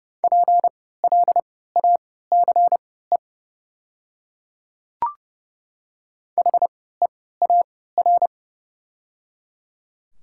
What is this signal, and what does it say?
Morse code, heard as a keyed tone, about 30 wpm, spaced as for 17 wpm, 690 Hz. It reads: PLACE HEAR
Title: 3x Character Spacing - Top 500 Words - 30wpm